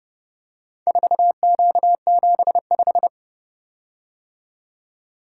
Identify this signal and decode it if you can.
Morse code, heard as a keyed tone, about 29 wpm, 700 Hz. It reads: 4Q75